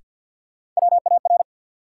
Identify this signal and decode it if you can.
Morse code, heard as a keyed tone, about 53 wpm, 700 Hz. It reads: WAR
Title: Sets of 5 Words - New Word Drill - 50wpm